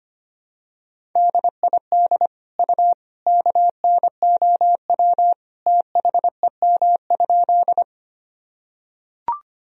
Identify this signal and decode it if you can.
Morse code, heard as a keyed tone, about 25 wpm, 700 Hz. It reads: DID U KNOW THEM?
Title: Sentences from Top 100 Words and Abbreviations 25wpm